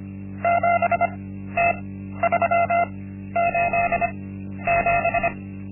Morse code, heard as a keyed tone, about 25 wpm, spaced as for 12 wpm, 690 Hz. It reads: 7T387
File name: Linksmakalnis-1